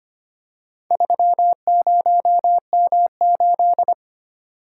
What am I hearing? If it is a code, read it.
Morse code, heard as a keyed tone, about 25 wpm, 690 Hz. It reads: 30M8
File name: practice / 25wpm / Four Letter-Number Combinations - 25wpm